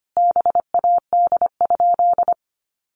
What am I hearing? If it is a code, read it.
Morse code, heard as a keyed tone, about 25 wpm, 700 Hz. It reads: BAD?
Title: Sentences from Top 800 Words 25wpm